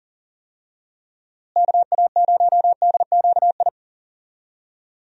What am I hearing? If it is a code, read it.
Morse code, heard as a keyed tone, about 40 wpm, 700 Hz. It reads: KA0DQI
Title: USA Call Signs - 6 Characters - Repeated 2x - 40wpm